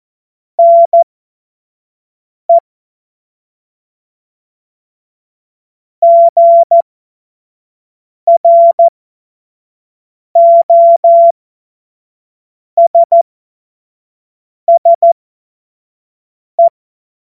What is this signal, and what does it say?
Morse code, heard as a keyed tone, about 14 wpm, spaced as for 5 wpm, 680 Hz. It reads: NE GROSSE
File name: Morsekurs / Witze / Witz 16. Februar 2021, 25 F70